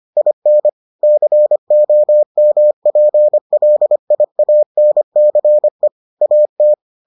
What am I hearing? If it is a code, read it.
Morse code, heard as a keyed tone, about 25 wpm, 600 Hz. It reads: IN COMPLIANCE AT